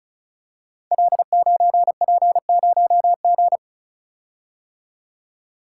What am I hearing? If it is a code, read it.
Morse code, heard as a keyed tone, about 36 wpm, 690 Hz. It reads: L9P0G
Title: Code Groups - 5 Letter:Numbers - 35wpm